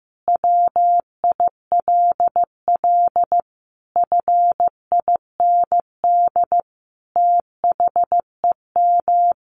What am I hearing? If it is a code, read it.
Morse code, heard as a keyed tone, about 15 wpm, 700 Hz. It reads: WILL FIND THEM
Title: Sentences from Top 700 Words 15wpm